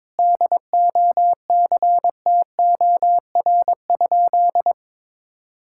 Morse code, heard as a keyed tone, about 22 wpm, 700 Hz. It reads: DOCTOR?
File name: practice / 22wpm / Sentences with 4-Word Phrases - 22wpm